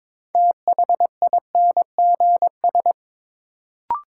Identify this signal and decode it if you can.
Morse code, heard as a keyed tone, about 23 wpm, 700 Hz. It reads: THINGS
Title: Sentences from Top 1400 Words - New Word Drill Alt - 22wpm